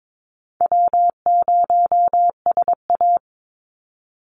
Morse code, heard as a keyed tone, about 22 wpm, 700 Hz. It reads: W0SA
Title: Call Signs - 22wpm